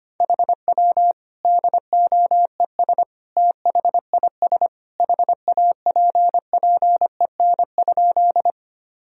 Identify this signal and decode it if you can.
Morse code, heard as a keyed tone, about 25 wpm, 700 Hz. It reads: HW DOES THIS HAPPEN?
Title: Sentences from Top 400 Words and Abbreviations 25wpm